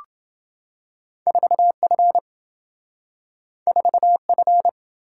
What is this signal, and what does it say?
Morse code, heard as a keyed tone, about 27 wpm, 700 Hz. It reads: 4F 4F